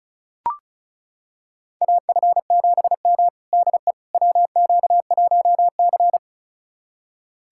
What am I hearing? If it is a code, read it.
Morse code, heard as a keyed tone, about 36 wpm, 700 Hz. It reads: AF7M DE WQ1C
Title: USA Callsign DE Callsign - 4 Characters - 35wpm